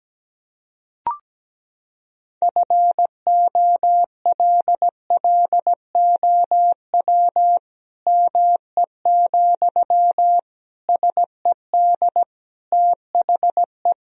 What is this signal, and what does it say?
Morse code, heard as a keyed tone, about 17 wpm, 700 Hz. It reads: FOLLOW ME, SED THE
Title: Sentences from Top 200 Words and Abbreviations 17wpm